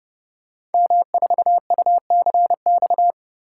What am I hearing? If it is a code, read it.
Morse code, heard as a keyed tone, about 28 wpm, 700 Hz. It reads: M4UCX
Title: Code Groups - 5 Letter:Numbers - 30wpm